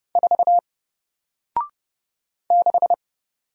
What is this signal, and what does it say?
Morse code, heard as a keyed tone, about 31 wpm, 700 Hz. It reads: 4 6
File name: Single-Letter-Number-30wpm